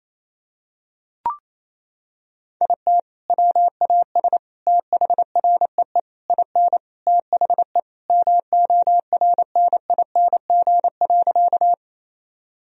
Morse code, heard as a keyed tone, about 28 wpm, 700 Hz. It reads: IT WAS THREE IN THE MORNING.
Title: Sentences from Top 1000 Words - Review Part 1 - 28wpm